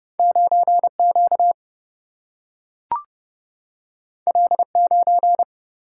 Morse code, heard as a keyed tone, about 31 wpm, 700 Hz. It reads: 9 Q L 9
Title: Single-Letter Single-Number 30wpm